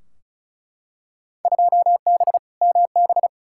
Morse code, heard as a keyed tone, about 36 wpm, 700 Hz. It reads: 2B MB
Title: Field Day Exchanges Encore - 35wpm